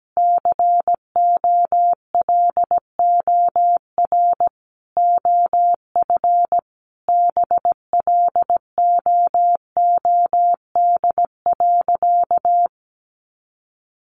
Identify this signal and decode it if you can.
Morse code, heard as a keyed tone, about 17 wpm, 700 Hz. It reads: COLOR OF BLOOD.
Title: A Sound Of Thunder 17wpm